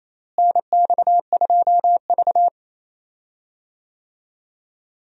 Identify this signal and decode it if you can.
Morse code, heard as a keyed tone, about 28 wpm, 700 Hz. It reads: NX2V
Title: USA Call Signs - 4 Characters - Repeated 3x - 28wpm